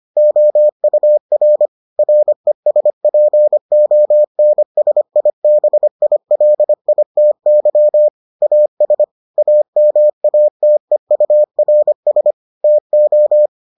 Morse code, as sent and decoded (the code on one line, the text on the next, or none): --- ..- .-. / .-. ... .--. --- -. ... .. -... .. .-.. .. - -.-- / .- ... / .- -- .- - . ..- .-. ... / - ---
OUR RESPONSIBILITY AS AMATEURS TO